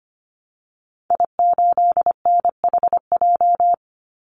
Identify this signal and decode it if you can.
Morse code, heard as a keyed tone, about 25 wpm, 700 Hz. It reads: I8NHJ
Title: Call Signs - 25wpm